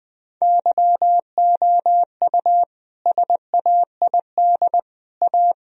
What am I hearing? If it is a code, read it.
Morse code, heard as a keyed tone, about 20 wpm, 700 Hz. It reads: YOU SAID A
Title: Sentences from Top 1000 Words - Review Part 3 - 20wpm